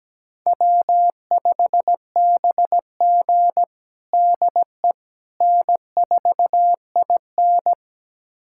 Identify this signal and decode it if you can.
Morse code, heard as a keyed tone, about 17 wpm, 700 Hz. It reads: W5BG DE N4IN
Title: USA Callsign DE Callsign - 4 Characters - 17wpm